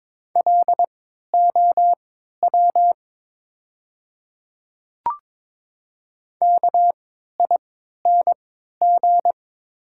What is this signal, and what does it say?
Morse code, heard as a keyed tone, about 22 wpm, spaced as for 12 wpm, 700 Hz. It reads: LOW KING
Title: 3x Character Spacing - Top 500 Words - 22wpm